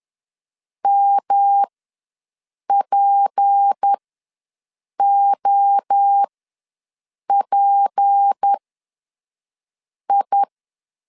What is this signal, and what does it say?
Morse code, heard as a keyed tone, about 10 wpm, spaced as for 5 wpm, 800 Hz. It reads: MPOPI
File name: LEZIONI / 6 PKX0 / 6 P ws 10wpm 3751